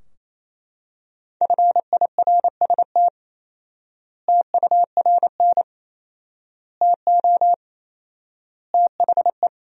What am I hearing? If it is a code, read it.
Morse code, heard as a keyed tone, about 28 wpm, 700 Hz. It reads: FIRST TURN TO THE
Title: 4x Word-Spacing - Sentences from Top 300 Words - 28wpm